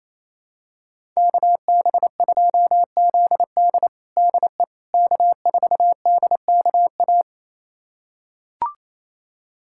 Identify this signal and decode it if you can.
Morse code, heard as a keyed tone, about 28 wpm, 700 Hz. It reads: KB2ZD DE K4DKA
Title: USA Callsign DE Callsign - 5 Characters - 28wpm